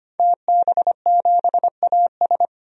Morse code, heard as a keyed tone, about 25 wpm, 700 Hz. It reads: TB7AS
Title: DXCC-100 Call Signs - 5 Characters - 25wpm